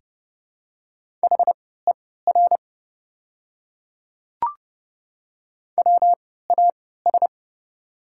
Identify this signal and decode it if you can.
Morse code, heard as a keyed tone, about 30 wpm, spaced as for 17 wpm, 700 Hz. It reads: HER WAS
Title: 3x Character Spacing - Top 100 Words - 30wpm